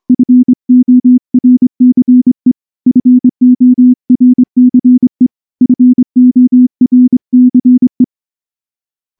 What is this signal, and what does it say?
Morse code, heard as a keyed tone, about 23 wpm, 250 Hz. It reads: FORCE FORCE FORCE
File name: morse / morse-training-25WPM